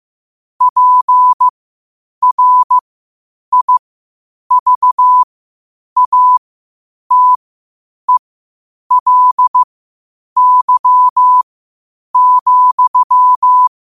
Morse code, heard as a keyed tone, about 15 wpm, spaced as for 8 wpm, 1000 Hz. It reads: PRIVATELY,
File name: qotdfiles / 8wpm